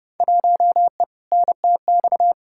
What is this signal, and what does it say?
Morse code, heard as a keyed tone, about 32 wpm, 700 Hz. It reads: 1E NTX
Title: Field Day Exchanges - 30wpm